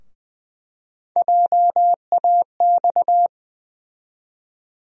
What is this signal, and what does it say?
Morse code, heard as a keyed tone, about 20 wpm, 700 Hz. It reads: JAX
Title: USA Call Sign Postfixes - 3 Letters 20wpm